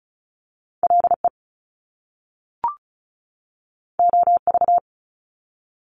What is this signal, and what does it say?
Morse code, heard as a keyed tone, about 36 wpm, 700 Hz. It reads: LE OV
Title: Top Two Letter Combinations in Words 35wpm